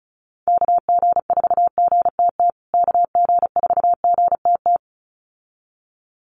Morse code, heard as a keyed tone, about 34 wpm, 700 Hz. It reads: KG4GTT KG4GTT
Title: USA Call Signs Repeated 2x 35wpm